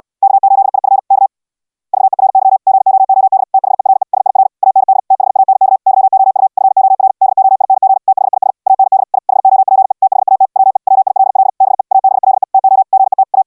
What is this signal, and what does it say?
Morse code, heard as a keyed tone, about 69 wpm, 770 Hz. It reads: K8IF K VERY GOOD THERE SIR AND THANK YOU FOR DOING THIS AND I HOPE THAT WE WILL BE ABLE TO GET A